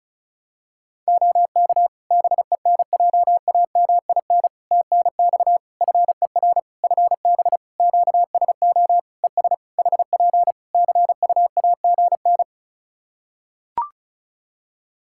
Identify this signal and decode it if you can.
Morse code, heard as a keyed tone, about 36 wpm, 700 Hz. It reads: OK BENJAMIN TNX FER FB QSO ES HP CUAGN
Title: Practice QSO - Exchange 4 - Ending - 35wpm